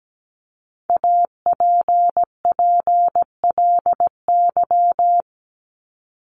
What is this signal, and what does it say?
Morse code, heard as a keyed tone, about 17 wpm, 700 Hz. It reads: APPLY
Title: ED Suffix - Top 100 Words with Base Words 17wpm